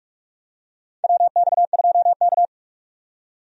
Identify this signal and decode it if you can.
Morse code, heard as a keyed tone, about 46 wpm, 700 Hz. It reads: WX2K
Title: USA Call Signs - 4 Characters - 45wpm